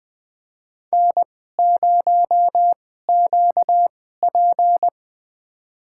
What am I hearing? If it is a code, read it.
Morse code, heard as a keyed tone, about 20 wpm, spaced as for 14 wpm, 700 Hz. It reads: N0QP
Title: USA Call Signs - 4 Characters - 2x Character Spacing - 20wpm